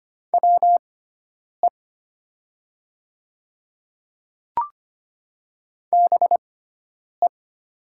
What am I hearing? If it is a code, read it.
Morse code, heard as a keyed tone, about 25 wpm, spaced as for 9 wpm, 700 Hz. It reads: WE BE
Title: 6x Character Spacing - Top 100 Words - 25wpm